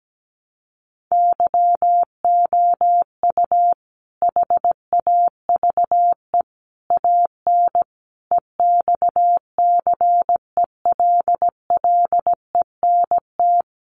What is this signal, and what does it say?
Morse code, heard as a keyed tone, about 17 wpm, 700 Hz. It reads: YOU HAVE AN EXCELLENT
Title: Sentences from Top 1300 Words 17wpm